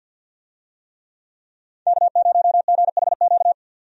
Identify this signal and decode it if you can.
Morse code, heard as a keyed tone, about 51 wpm, 700 Hz. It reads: K0GHQ
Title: USA Call Signs - 5 Characters - 50wpm